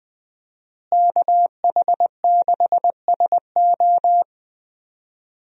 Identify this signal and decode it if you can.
Morse code, heard as a keyed tone, about 20 wpm, 700 Hz. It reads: KH6SO
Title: USA Call Signs - 5 Characters - Repeated 3x - 20wpm